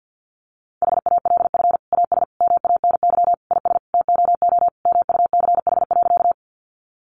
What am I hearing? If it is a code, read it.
Morse code, heard as a keyed tone, about 50 wpm, 700 Hz. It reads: HALF AS MANY IS TOO MUCH.
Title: Sentences from Top 400 Words 50wpm